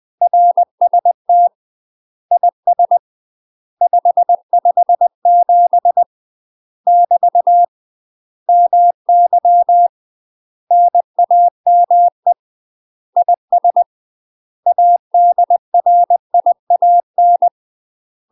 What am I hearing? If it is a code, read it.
Morse code, heard as a keyed tone, about 20 wpm, 700 Hz. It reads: RST IS 557 <BT> MY NAME IS ADRIAN
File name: UK MASTER QSOs TRAINING / 20WPM / UK MASTER QSOs 0003